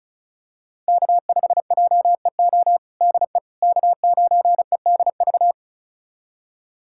Morse code, heard as a keyed tone, about 36 wpm, 700 Hz. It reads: K5JEO DE K9EDV
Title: USA Callsign DE Callsign - 5 Characters - 35wpm